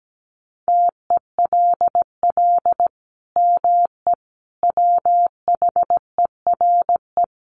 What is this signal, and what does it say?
Morse code, heard as a keyed tone, about 17 wpm, 700 Hz. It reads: TELL ME WHERE
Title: Sentences from Top 1500 Words 17wpm